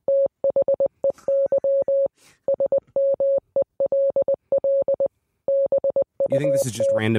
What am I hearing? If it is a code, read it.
Morse code, heard as a keyed tone, about 20 wpm, 550 Hz, with speech. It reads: THEY SMELL BUT